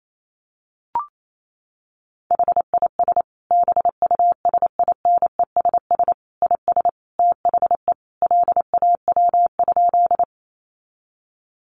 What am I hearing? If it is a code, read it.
Morse code, heard as a keyed tone, about 28 wpm, 700 Hz. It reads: HIS BUSINESS IS THE LAW?
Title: Sentences from Top 600 Words 28wpm